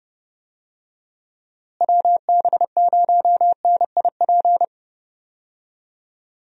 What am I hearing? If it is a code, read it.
Morse code, heard as a keyed tone, about 30 wpm, 700 Hz. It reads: WB0NIP
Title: USA Call Signs - 6 Characters - Repeated 2x - 30wpm